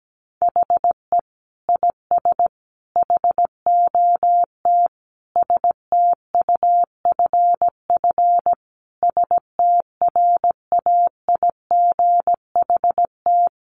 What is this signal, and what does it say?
Morse code, heard as a keyed tone, about 17 wpm, 700 Hz. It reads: HE IS HOT STUFF STRAIGHT
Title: Sentences from Top 1500 Words 17wpm